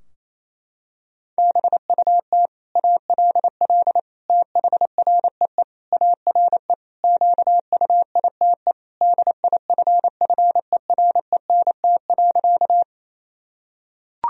Morse code, heard as a keyed tone, about 27 wpm, 700 Hz. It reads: BUT ALL THREE ARE QUITE DIFFERENT.
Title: Sentences from Top 1000 Words - Review Part 3 - 28wpm